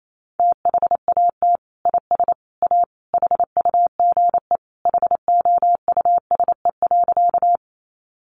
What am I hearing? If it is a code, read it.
Morse code, heard as a keyed tone, about 28 wpm, 700 Hz. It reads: THAT IS A HUGE HOUSE.
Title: Sentences from Top 1100 Words 28wpm